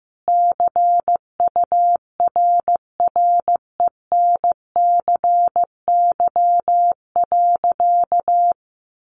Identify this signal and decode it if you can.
Morse code, heard as a keyed tone, about 15 wpm, 700 Hz. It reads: CURRENCY.
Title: Sentences from Top 1500 Words 15wpm